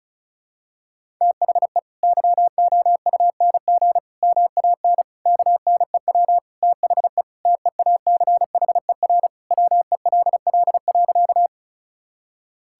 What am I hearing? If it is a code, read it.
Morse code, heard as a keyed tone, about 36 wpm, 700 Hz. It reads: THE YOUNG MAN KNEW THE TEACHER WELL.
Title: Sentences from Top 1000 Words - Review Part 6 - 35wpm